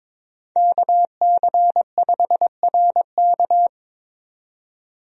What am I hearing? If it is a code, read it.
Morse code, heard as a keyed tone, about 22 wpm, 700 Hz. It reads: KC5RK
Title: USA Call Signs - 5 Characters - 22wpm